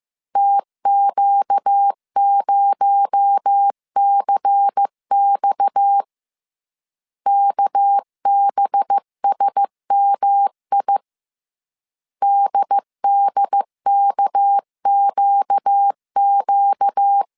Morse code, heard as a keyed tone, about 15 wpm, 800 Hz. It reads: TQ0CX KBSMI DDKQQ
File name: CW / DET / 1 6 1500C S C 14wpm 2951